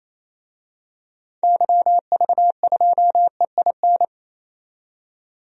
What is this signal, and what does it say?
Morse code, heard as a keyed tone, about 27 wpm, 700 Hz. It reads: YV2EIN
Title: DXCC-100 Call Signs - 6 Characters - 28wpm